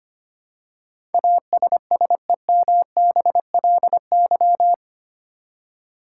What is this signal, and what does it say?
Morse code, heard as a keyed tone, about 25 wpm, 700 Hz. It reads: ASSEMBLY
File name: 8 Letter Words - First 3 Letters Given - 25wpm